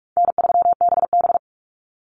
Morse code, heard as a keyed tone, about 46 wpm, 700 Hz. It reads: N3BB
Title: Easy Call Signs - 45wpm